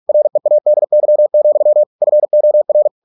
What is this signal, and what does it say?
Morse code, heard as a keyed tone, about 46 wpm, 600 Hz. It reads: READY, FOR